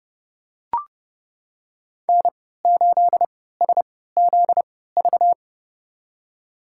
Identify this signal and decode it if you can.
Morse code, heard as a keyed tone, about 30 wpm, spaced as for 17 wpm, 700 Hz. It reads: N8SZV
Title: USA Call Signs - 5 Characters - 3x Character Spacing - 30wpm